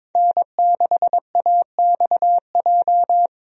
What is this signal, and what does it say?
Morse code, heard as a keyed tone, about 22 wpm, 700 Hz. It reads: N6AXJ
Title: USA Call Signs - 5 Characters - Repeated 2x - 22wpm